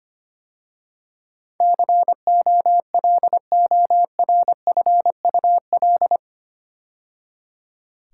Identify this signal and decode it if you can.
Morse code, heard as a keyed tone, about 25 wpm, 700 Hz. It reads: COLORFUL
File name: 8 Letter Words - First 3 Letters Given - 25wpm